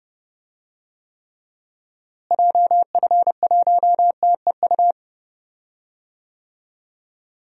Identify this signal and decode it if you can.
Morse code, heard as a keyed tone, about 30 wpm, 700 Hz. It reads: JF1TEU
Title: DXCC-100 Call Signs - 6 Characters - Repeated 2x - 30wpm